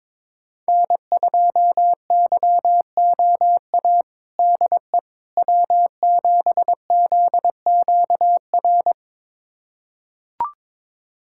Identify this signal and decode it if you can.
Morse code, heard as a keyed tone, about 21 wpm, 700 Hz. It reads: N2YOA DE W7ZQR
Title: USA Callsign DE Callsign - 5 Characters - 22wpm